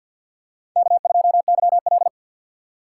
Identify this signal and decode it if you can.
Morse code, heard as a keyed tone, about 48 wpm, 700 Hz. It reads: K2YL
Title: USA Call Signs - 4x Character Spacing - 50wpm